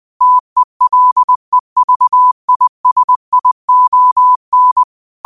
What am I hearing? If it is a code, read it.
Morse code, heard as a keyed tone, about 20 wpm, 1000 Hz. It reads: TELEVISION
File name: qotdfiles / 20wpm